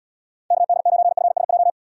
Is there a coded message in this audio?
Yes: Morse code from 0.5 to 1.7 s, heard as a keyed tone, about 78 wpm, 700 Hz. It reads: KD9FSJ